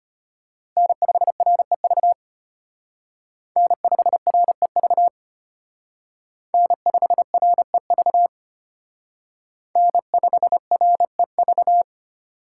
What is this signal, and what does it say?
Morse code, heard as a keyed tone, about 33 wpm, 700 Hz. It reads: N5REV N5REV N5REV N5REV